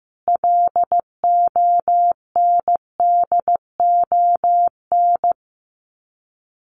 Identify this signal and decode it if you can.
Morse code, heard as a keyed tone, about 15 wpm, 700 Hz. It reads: LONDON
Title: Common QSO Phrases 15wpm